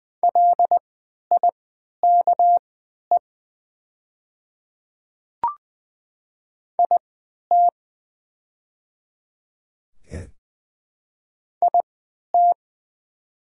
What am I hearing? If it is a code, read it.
Morse code, heard as a keyed tone, about 20 wpm, spaced as for 11 wpm, 700 Hz. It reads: LIKE IT IT
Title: 3x Character Spacing - Top 100 Words - 20wpm